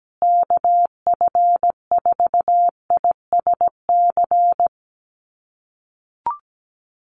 Morse code, heard as a keyed tone, about 17 wpm, 700 Hz. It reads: KF4ISC
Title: USA Call Signs Repeated 3x 17wpm